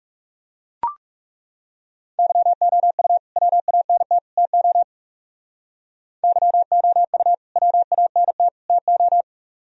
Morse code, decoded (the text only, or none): YOU WANT TO YOU WANT TO